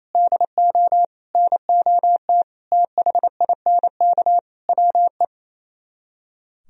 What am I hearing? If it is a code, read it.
Morse code, heard as a keyed tone, about 32 wpm, 700 Hz. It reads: DO NOT THINK WE
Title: Speed-Racing - Sets of 5 Words 28wpm